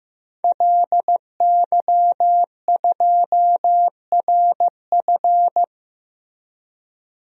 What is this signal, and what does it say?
Morse code, heard as a keyed tone, about 16 wpm, 700 Hz. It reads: LY2RF
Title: DXCC-100 Call Signs - 5 Characters - Repeated 2x - 15wpm